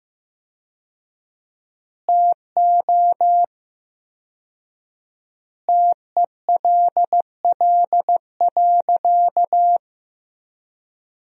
Morse code, heard as a keyed tone, about 15 wpm, 700 Hz. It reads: TO TELL.
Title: 4x Word-Spacing - Sentences from Top 200 Words - 15wpm